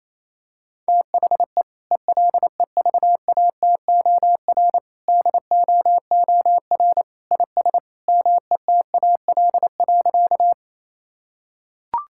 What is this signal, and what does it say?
Morse code, heard as a keyed tone, about 28 wpm, 700 Hz. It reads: THE ELEVATOR DOOR IS METAL.
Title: Sentences from Top 1600 Words 28wpm